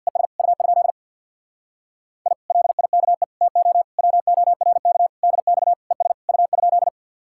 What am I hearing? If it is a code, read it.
Morse code, heard as a keyed tone, about 60 wpm, 700 Hz. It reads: ES U? I LIKE TO WORK DX ES U?
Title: Speed-Racing Common QSO Phrases - 50wpm